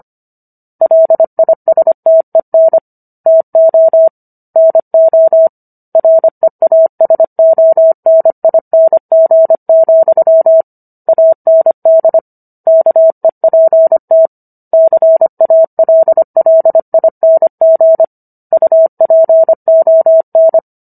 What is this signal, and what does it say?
Morse code, heard as a keyed tone, about 25 wpm, 650 Hz. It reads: LISTEN TO NO REASONING, AND KEPT CALLING UPON